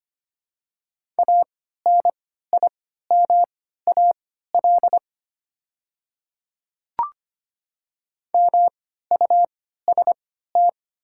Morse code, heard as a keyed tone, about 25 wpm, spaced as for 14 wpm, 700 Hz. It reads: ANIMAL MUST
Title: 3x Character Spacing - Top 200 Words - 25wpm